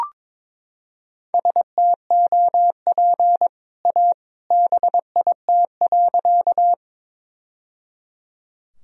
Morse code, heard as a keyed tone, about 22 wpm, 700 Hz. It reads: STOP A BIT.
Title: Sentences from Top 1000 Words - Review Part 1 - 22wpm